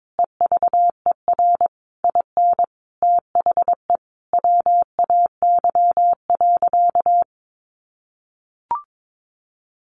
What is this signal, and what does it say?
Morse code, heard as a keyed tone, about 22 wpm, 700 Hz. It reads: EVER IN THE WAY.
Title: Sentences from Top 400 Words 22wpm